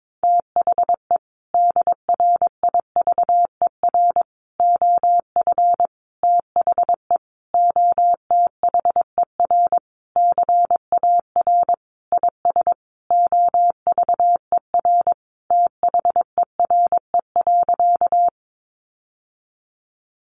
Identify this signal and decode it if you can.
Morse code, heard as a keyed tone, about 22 wpm, 700 Hz. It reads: THE DRIVER OF THE OTHER CAR IS OVER THERE.
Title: Sentences from Top 1000 Words 22wpm